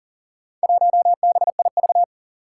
Morse code, heard as a keyed tone, about 39 wpm, 700 Hz. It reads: 1BIV
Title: Four Challenging Letter-Number Combinations 40wpm